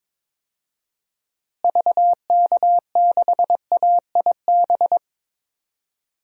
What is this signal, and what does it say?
Morse code, heard as a keyed tone, about 22 wpm, 700 Hz. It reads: VK6AIB
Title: DXCC-100 Call Signs - 6 Characters - Repeated 2x - 22wpm